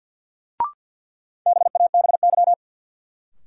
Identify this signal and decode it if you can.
Morse code, heard as a keyed tone, about 51 wpm, 700 Hz. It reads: BABY
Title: Sentences from Top 1300 Words - New Word Drill Alt - 50wpm